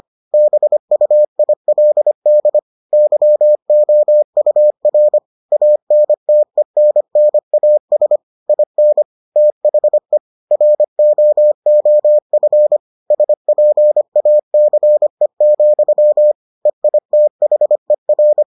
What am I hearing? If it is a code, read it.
Morse code, heard as a keyed tone, about 25 wpm, 600 Hz. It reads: BUILD YOUR ANTENNAS IN THE ROOF SPACE, EITHER